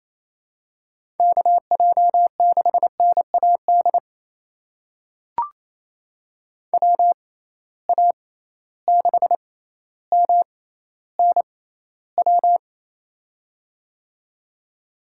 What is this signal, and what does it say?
Morse code, heard as a keyed tone, about 28 wpm, spaced as for 14 wpm, 700 Hz. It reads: KJ6NAD WA6MNW
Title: USA Call Signs - 6 Characters - 6x Character Spacing - 28wpm